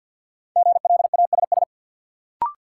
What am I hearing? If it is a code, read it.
Morse code, heard as a keyed tone, about 49 wpm, 700 Hz. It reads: GLASS
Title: Sentences from Top 1000 Words - Review Part 3 - New Word Drill - 50wpm